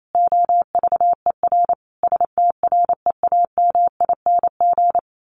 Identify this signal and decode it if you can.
Morse code, heard as a keyed tone, about 28 wpm, 700 Hz. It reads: OVER STREAMING
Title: A Sound Of Thunder 28wpm